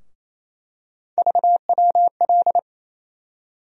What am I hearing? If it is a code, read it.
Morse code, heard as a keyed tone, about 28 wpm, 700 Hz. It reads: VWL
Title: USA Call Sign Postfixes - 3 Letters 28wpm